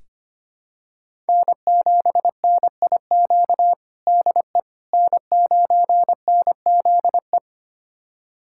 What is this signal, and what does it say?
Morse code, heard as a keyed tone, about 25 wpm, 700 Hz. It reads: N7NIQ DE N9NZE
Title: USA Callsign DE Callsign - 5 Characters - 25wpm